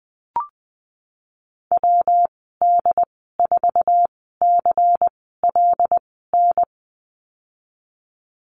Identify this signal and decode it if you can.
Morse code, heard as a keyed tone, about 20 wpm, spaced as for 15 wpm, 700 Hz. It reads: WD4CLN
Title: 167 Lesson 27g - Number 7 - Call Signs 20wpm